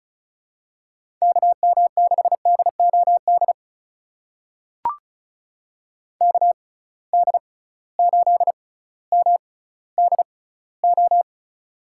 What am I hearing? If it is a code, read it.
Morse code, heard as a keyed tone, about 34 wpm, spaced as for 18 wpm, 700 Hz. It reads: KM6DOD KD8MDO
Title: USA Call Signs - 6 Characters - 6x Character Spacing - 35wpm